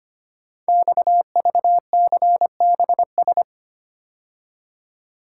Morse code, heard as a keyed tone, about 25 wpm, 700 Hz. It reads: XVCBS
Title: Code Groups - 5 Letters - 25wpm